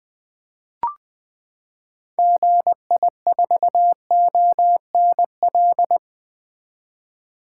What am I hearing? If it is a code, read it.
Morse code, heard as a keyed tone, about 20 wpm, 700 Hz. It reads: GI4ONL
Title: 2023 SOTA Activator Call Signs - 20wpm